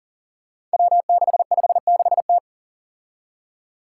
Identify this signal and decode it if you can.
Morse code, heard as a keyed tone, about 40 wpm, 700 Hz. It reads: W656T
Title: Five Challenging Letter-Number Combinations 40wpm